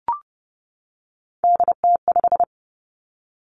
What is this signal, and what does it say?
Morse code, heard as a keyed tone, about 30 wpm, 700 Hz. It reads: DT5
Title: Three Challenging Letter-Number Combinations 30wpm